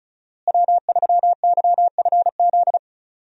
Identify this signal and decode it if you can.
Morse code, heard as a keyed tone, about 36 wpm, 700 Hz. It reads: W3YFZ
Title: USA Call Signs - 5 Characters - Repeated 3x - 35wpm